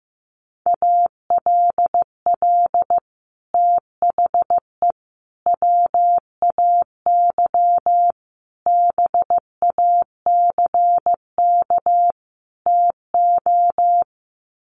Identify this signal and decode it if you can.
Morse code, heard as a keyed tone, about 15 wpm, 700 Hz. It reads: ALL THE WAY BACK TO
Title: Sets of 5 Words Encore 15wpm